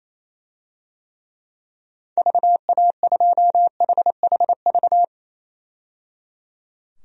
Morse code, heard as a keyed tone, about 28 wpm, 700 Hz. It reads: VA2HHV
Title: DXCC-100 Call Signs - 6 Characters - 28wpm